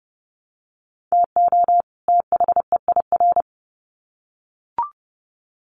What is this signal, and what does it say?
Morse code, heard as a keyed tone, about 30 wpm, 700 Hz. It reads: TO THEIR
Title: Sets of 2 Words 30wpm